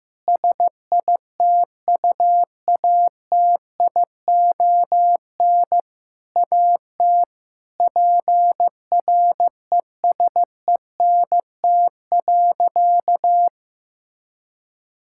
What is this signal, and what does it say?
Morse code, heard as a keyed tone, about 15 wpm, 700 Hz. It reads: SITUATION AT PRESENT.
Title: Sentences from Top 1600 Words 15wpm